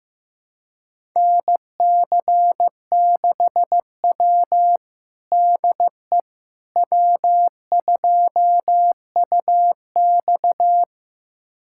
Morse code, heard as a keyed tone, about 15 wpm, 700 Hz. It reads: NC6W DE W2UX
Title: USA Callsign DE Callsign - 4 Characters - 15wpm